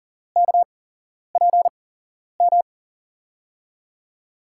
Morse code, heard as a keyed tone, about 40 wpm, spaced as for 11 wpm, 700 Hz. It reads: KPM
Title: USA Call Signs - 6 Characters - 8x Character Spacing - 40wpm